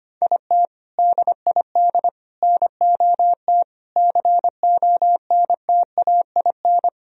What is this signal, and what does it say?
Morse code, heard as a keyed tone, about 25 wpm, 700 Hz. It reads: IT DID NOT CONTAIN
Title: Sentences from Top 1000 Words - Review Part 2 - 25wpm